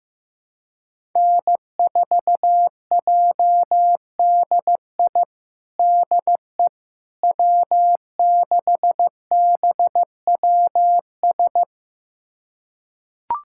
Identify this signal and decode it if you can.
Morse code, heard as a keyed tone, about 15 wpm, 700 Hz. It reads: N4JDI DE W6BWS
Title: USA Callsign DE Callsign - 5 Characters - 15wpm